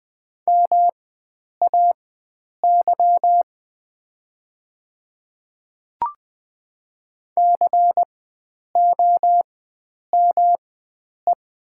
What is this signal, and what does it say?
Morse code, heard as a keyed tone, about 20 wpm, spaced as for 9 wpm, 700 Hz. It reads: MAY COME
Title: 4x Character Spacing - Top 100 Words - 20wpm